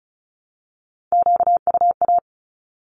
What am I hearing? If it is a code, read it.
Morse code, heard as a keyed tone, about 34 wpm, 700 Hz. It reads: QUA
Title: Q-Codes Repeated 3x 35wpm